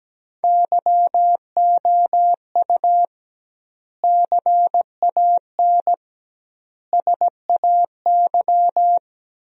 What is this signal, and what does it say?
Morse code, heard as a keyed tone, about 17 wpm, 700 Hz. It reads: YOU CAN SAY
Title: 2x Word-Spacing - Sentences from Top 200 Words - 17wpm